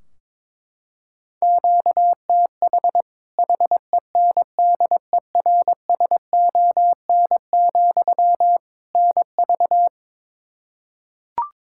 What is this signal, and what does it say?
Morse code, heard as a keyed tone, about 21 wpm, 700 Hz. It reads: QTH HENDERSON, NV
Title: Practice QSO - Exchange 1 - Beginning - 22wpm